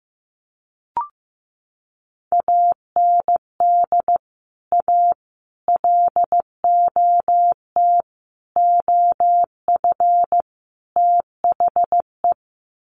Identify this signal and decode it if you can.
Morse code, heard as a keyed tone, about 15 wpm, 700 Hz. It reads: AND A LOT OF THE
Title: Sets of 5 Words Encore 15wpm